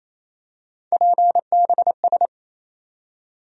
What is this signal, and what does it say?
Morse code, heard as a keyed tone, about 28 wpm, 700 Hz. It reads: PBS
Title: USA Call Sign Postfixes - 3 Letters 28wpm